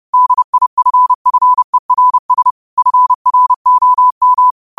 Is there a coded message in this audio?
Yes: Morse code from 0.1 to 4.5 s, heard as a keyed tone, about 30 wpm, 1000 Hz. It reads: DIFFERS FROM